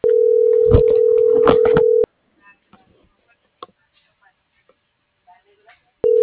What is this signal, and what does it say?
Ringback tone (North America), 440 + 480 Hz.